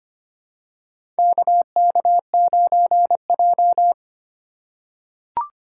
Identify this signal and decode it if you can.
Morse code, heard as a keyed tone, about 25 wpm, 700 Hz. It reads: KK9J
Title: USA Call Signs - 4 Characters - 3x Character Spacing - 25wpm